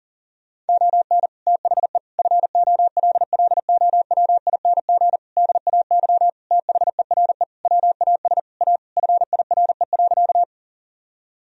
Morse code, heard as a keyed tone, about 40 wpm, 700 Hz. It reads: ON THE FOLLOWING DAY THERE WAS A FIRE.
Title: Sentences from Top 1000 Words - Review Part 4 - 40wpm